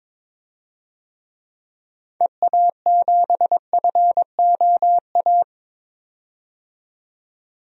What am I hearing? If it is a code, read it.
Morse code, heard as a keyed tone, about 21 wpm, 700 Hz. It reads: EA7FOA
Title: DXCC-100 Call Signs - 6 Characters - 22wpm